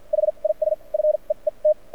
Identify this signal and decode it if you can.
Morse code, heard as a keyed tone, about 55 wpm, spaced as for 38 wpm, 610 Hz. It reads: HIS FEET
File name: archived / CRSnet